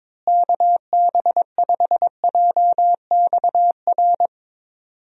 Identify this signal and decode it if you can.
Morse code, heard as a keyed tone, about 22 wpm, 700 Hz. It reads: KB5JXR
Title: USA Call Signs - 6 Characters - Repeated 3x - 22wpm